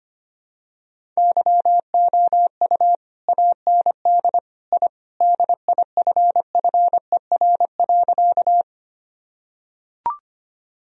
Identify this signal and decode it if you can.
Morse code, heard as a keyed tone, about 25 wpm, 700 Hz. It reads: YOU AND I DIFFER.
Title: Sentences from Top 1000 Words - Review Part 1 - 25wpm